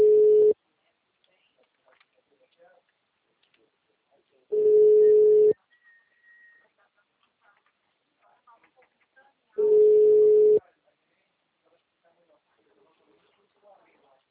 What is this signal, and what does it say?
Ringback tone (Europe, 425 Hz), with speech.
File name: out- -2013- - - .95712